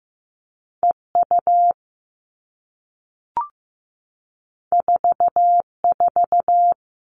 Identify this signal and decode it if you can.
Morse code, heard as a keyed tone, about 15 wpm, 700 Hz. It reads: EU 44
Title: All Cut Numbers - 2-Digit 15wpm